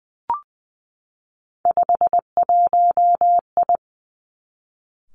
Morse code, heard as a keyed tone, about 19 wpm, 700 Hz. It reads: 51I
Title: Three Challenging Letter-Number Combinations 20wpm